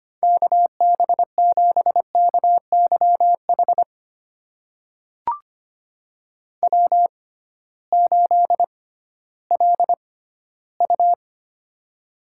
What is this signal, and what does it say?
Morse code, heard as a keyed tone, about 25 wpm, spaced as for 15 wpm, 700 Hz. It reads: KB7KYH W8LU
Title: USA Call Signs - 6x Character Spacing - 25wpm